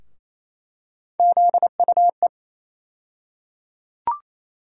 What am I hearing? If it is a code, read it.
Morse code, heard as a keyed tone, about 28 wpm, 700 Hz. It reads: ZUE
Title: USA Call Sign Postfixes - 3 Letters 28wpm